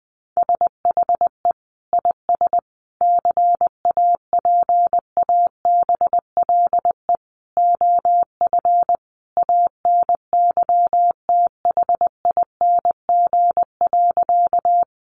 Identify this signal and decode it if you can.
Morse code, heard as a keyed tone, about 20 wpm, 700 Hz. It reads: SHE IS CAPABLE OF ANYTHING.
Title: Sentences from Top 1200 Words 20wpm